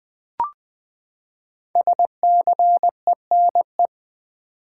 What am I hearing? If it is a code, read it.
Morse code, heard as a keyed tone, about 20 wpm, 700 Hz. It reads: SCENE
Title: Sentences from Top 1100 Words - New Word Drill v2 - 20wpm